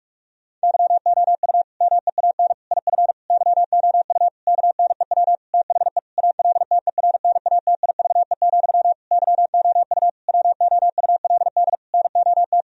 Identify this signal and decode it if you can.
Morse code, heard as a keyed tone, about 46 wpm, 700 Hz. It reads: YOU MEAN IF YOU KNEW THE ALTERNATIVE, YOU WOULD NOT
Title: Sentences from Top 1000 Words - Review Part 4 - 45wpm